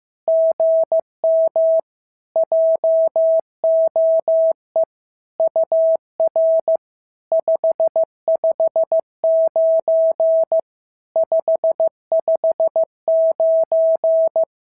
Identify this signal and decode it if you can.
Morse code, heard as a keyed tone, about 15 wpm, 650 Hz. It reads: GM JOE UR 559 559
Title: SOTA QSO Practice 15wpm